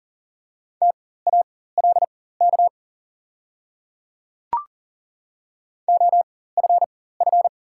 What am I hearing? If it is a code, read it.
Morse code, heard as a keyed tone, about 38 wpm, spaced as for 18 wpm, 700 Hz. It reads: TALK OFF
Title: 4x Character Spacing - Top 500 Words - 40wpm